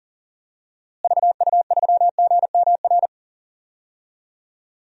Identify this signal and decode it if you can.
Morse code, heard as a keyed tone, about 40 wpm, 700 Hz. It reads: VU3GMR
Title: DXCC-100 Call Signs - 6 Characters - Repeated 2x - 40wpm